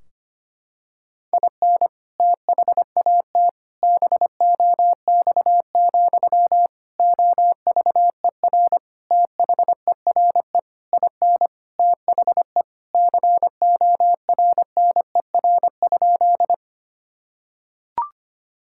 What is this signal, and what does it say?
Morse code, heard as a keyed tone, about 25 wpm, 700 Hz. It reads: IN THAT BOX, OVER THERE IN THE CORNER?
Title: Sentences from Top 1600 Words 25wpm